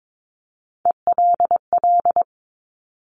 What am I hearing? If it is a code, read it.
Morse code, heard as a keyed tone, about 22 wpm, 700 Hz. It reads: ELL
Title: Top Three Letter Combinations in Words 22wpm